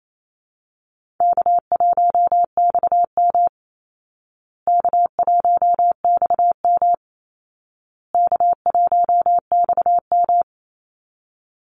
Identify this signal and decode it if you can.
Morse code, heard as a keyed tone, about 28 wpm, 700 Hz. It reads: K1XM K1XM K1XM
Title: Call Signs Repeated 2x - 28wpm